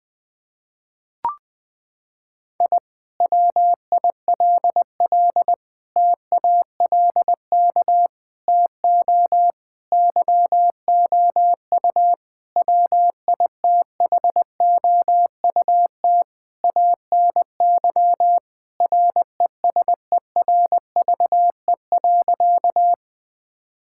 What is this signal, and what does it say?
Morse code, heard as a keyed tone, about 21 wpm, 700 Hz. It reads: I WILL TALK TO YOU WITHOUT ANY RESERVE.
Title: Sentences from Top 1000 Words - Review Part 6 - 20wpm